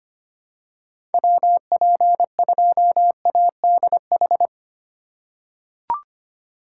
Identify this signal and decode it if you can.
Morse code, heard as a keyed tone, about 25 wpm, 700 Hz. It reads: WP2ADH
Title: USA Call Signs - 6 Characters - 3x Character Spacing - 25wpm